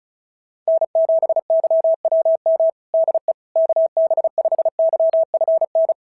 Morse code, heard as a keyed tone, about 36 wpm, 650 Hz. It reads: N7YWM DE KB5YFN
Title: SOTA QSO Practice 35wpm